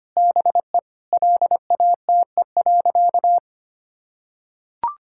Morse code, heard as a keyed tone, about 25 wpm, 700 Hz. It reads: BE LATE.
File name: practice / 25wpm / Sentences with 4-Word Phrases - 25wpm